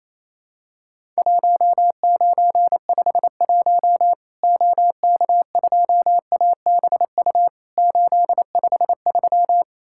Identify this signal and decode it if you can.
Morse code, heard as a keyed tone, about 28 wpm, 700 Hz. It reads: 1951 OK2ABU 853
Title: Worked All Europe DX Contest - QTC Exchange - 28wpm